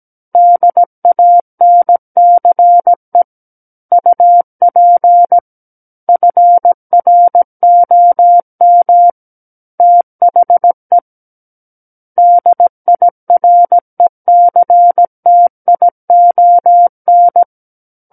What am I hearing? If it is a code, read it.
Morse code, heard as a keyed tone, about 17 wpm, 700 Hz. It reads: DANCE UP FROM THE DIRECTION
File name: Book One / 08-Chapter 8 - 17 WPM